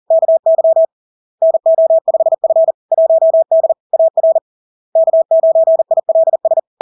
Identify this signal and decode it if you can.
Morse code, heard as a keyed tone, about 40 wpm, 650 Hz. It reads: KY NO5F 1D AR K9ILS
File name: LARK-FD-2022-Log-in-CW-40wpm-650-Hz